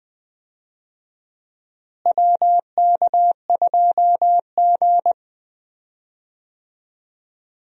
Morse code, heard as a keyed tone, about 20 wpm, 700 Hz. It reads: WK2G
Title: USA Call Signs - 4 Characters - Repeated 3x - 20wpm